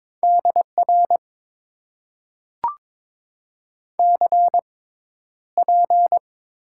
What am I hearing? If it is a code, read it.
Morse code, heard as a keyed tone, about 22 wpm, spaced as for 11 wpm, 700 Hz. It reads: DR CP
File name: Top 90 QSO Elements - 6x Character Spacing - 22wpm